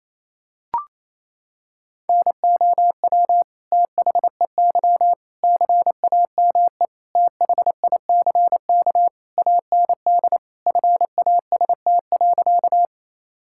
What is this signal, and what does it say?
Morse code, heard as a keyed tone, about 28 wpm, 700 Hz. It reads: NOW THEY CAME THICK AND FAST.
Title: Sentences from Top 1600 Words 28wpm